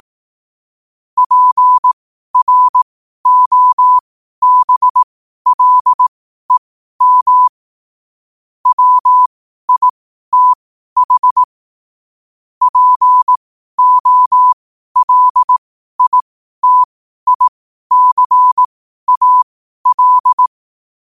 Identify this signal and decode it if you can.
Morse code, heard as a keyed tone, about 18 wpm, spaced as for 12 wpm, 1000 Hz. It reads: PROBLEM WITH POLITICAL
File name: qotdfiles / 12wpm